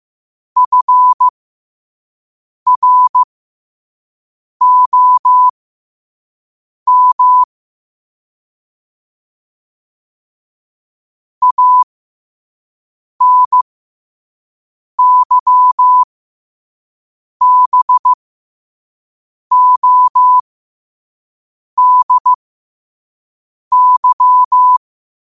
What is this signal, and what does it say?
Morse code, heard as a keyed tone, about 15 wpm, spaced as for 5 wpm, 1000 Hz. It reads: FROM ANYBODY
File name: qotdfiles / 5wpm